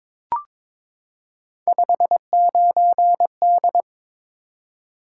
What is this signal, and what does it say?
Morse code, heard as a keyed tone, about 22 wpm, 700 Hz. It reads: 59D
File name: practice / 22wpm / Three Letter-Number Combinations - 22wpm